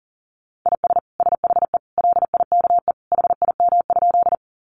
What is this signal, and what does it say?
Morse code, heard as a keyed tone, about 40 wpm, 700 Hz. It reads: IS SHE LIKE HIM?
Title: Sentences from Top 100 Words 40wpm